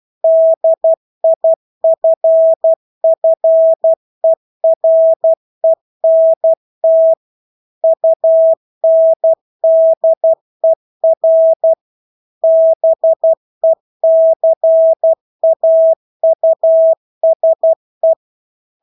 Morse code, heard as a keyed tone, about 12 wpm, 650 Hz. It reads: DIFFERENT UNDER BECAUSE
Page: Engelske ord 12 wpm